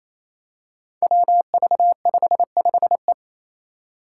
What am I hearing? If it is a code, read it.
Morse code, heard as a keyed tone, about 28 wpm, 700 Hz. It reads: WV55E
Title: Five Challenging Letter-Number Combinations 28wpm